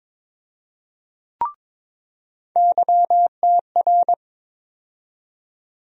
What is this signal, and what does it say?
Morse code, heard as a keyed tone, about 22 wpm, 700 Hz. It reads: YTR
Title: USA Call Sign Postfixes - 3 Letters 22wpm